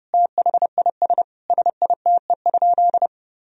Morse code, heard as a keyed tone, about 30 wpm, 700 Hz. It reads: THIS SITE?
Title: Sentences from Top 1000 Words - Review Part 4 - 30wpm